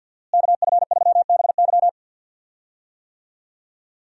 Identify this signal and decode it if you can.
Morse code, heard as a keyed tone, about 48 wpm, 700 Hz. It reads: KF3BY
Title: USA Call Signs - 5 Characters - Repeated 3x - 50wpm